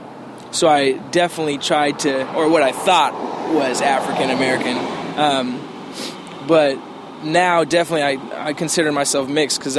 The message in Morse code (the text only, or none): none